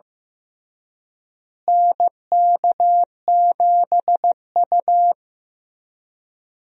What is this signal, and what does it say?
Morse code, heard as a keyed tone, about 15 wpm, 700 Hz. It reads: NK7U
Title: USA Call Signs - 4 Characters - Repeated 3x - 15wpm